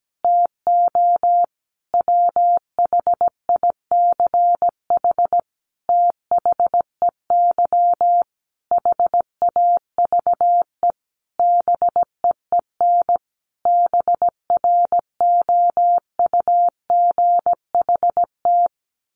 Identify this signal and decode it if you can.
Morse code, heard as a keyed tone, about 17 wpm, 700 Hz. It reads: TO WHICH THEY HAVE BEEN BROUGHT
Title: Sentences from Top 1500 Words 17wpm